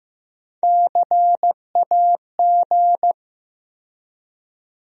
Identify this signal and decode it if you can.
Morse code, heard as a keyed tone, about 15 wpm, 700 Hz. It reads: CAG
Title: USA Call Sign Postfixes - 3 Letters 15wpm